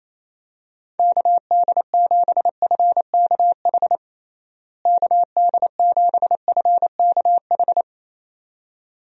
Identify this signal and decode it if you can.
Morse code, heard as a keyed tone, about 28 wpm, 700 Hz. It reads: KD7FKH KD7FKH
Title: USA Call Signs - 6 Characters - Repeated 2x - 28wpm